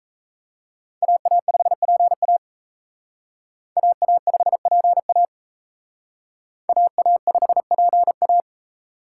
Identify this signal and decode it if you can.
Morse code, heard as a keyed tone, about 35 wpm, 700 Hz. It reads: AA5PA AA5PA AA5PA
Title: Speed Racing - USA 2x2 Call Signs 28wpm